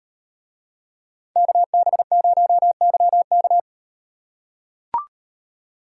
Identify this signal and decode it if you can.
Morse code, heard as a keyed tone, about 35 wpm, 700 Hz. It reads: KB0YK